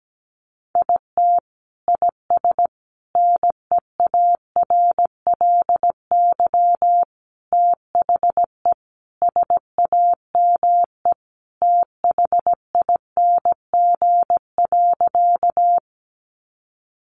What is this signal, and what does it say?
Morse code, heard as a keyed tone, about 17 wpm, 700 Hz. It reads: IT IS NEARLY THE SAME THING.
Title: Sentences from Top 1300 Words 17wpm